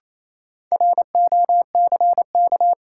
Morse code, heard as a keyed tone, about 28 wpm, 700 Hz. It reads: ROCK
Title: Sentences from Top 400 Words - New Word Drill - 28wpm